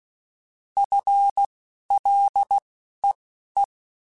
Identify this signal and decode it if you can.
Morse code, heard as a keyed tone, about 16 wpm, spaced as for 11 wpm, 800 Hz. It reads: FLEE
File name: FL / FL mp3s / FL Prefixes 3x2 Twice @10